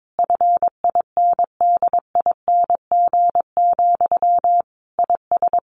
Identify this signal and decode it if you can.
Morse code, heard as a keyed tone, about 22 wpm, 700 Hz. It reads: FINDING, IS
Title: Sentences from Top 1100 Words 22wpm